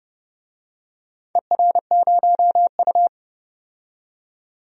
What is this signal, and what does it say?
Morse code, heard as a keyed tone, about 29 wpm, 700 Hz. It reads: ER0U
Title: DXCC-100 Call Signs - 4 Characters - 30wpm